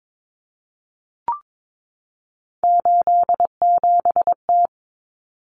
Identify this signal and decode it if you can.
Morse code, heard as a keyed tone, about 22 wpm, 700 Hz. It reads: 87T
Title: Common Cut Numbers - 3-Digit 22wpm